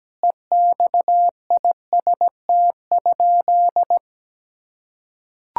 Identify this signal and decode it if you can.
Morse code, heard as a keyed tone, about 17 wpm, 700 Hz. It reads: EXIST?
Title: Sentences from Top 1000 Words - Review Part 5 - 17wpm